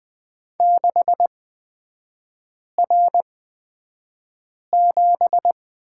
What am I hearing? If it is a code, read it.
Morse code, heard as a keyed tone, about 20 wpm, spaced as for 5 wpm, 700 Hz. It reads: 6R7
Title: 243 Lesson 39e - Letter z - Three Characters 20wpm